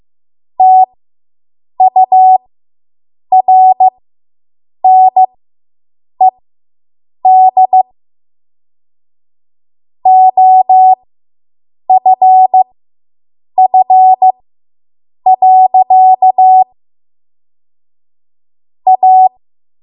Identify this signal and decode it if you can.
Morse code, heard as a keyed tone, about 15 wpm, spaced as for 7 wpm, 750 Hz. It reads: TURNED OFF. A